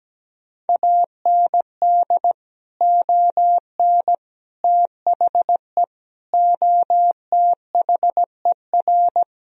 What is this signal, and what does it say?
Morse code, heard as a keyed tone, about 17 wpm, 700 Hz. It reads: AND ON THE OTHER
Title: Sentences from Top 1000 Words - Review Part 3 - 17wpm